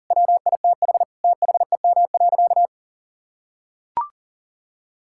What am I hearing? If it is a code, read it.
Morse code, heard as a keyed tone, about 37 wpm, 700 Hz. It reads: WITH THEM.
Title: Sentences from Top 1600 Words 40wpm